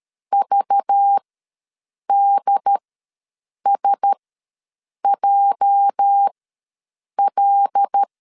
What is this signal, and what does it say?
Morse code, heard as a keyed tone, about 13 wpm, spaced as for 7 wpm, 800 Hz. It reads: VDSJL
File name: LEZIONI / 8 YJ45 / 8 YJ ws 12wpm 3342